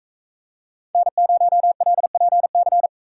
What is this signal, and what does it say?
Morse code, heard as a keyed tone, about 43 wpm, 700 Hz. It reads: N0LPC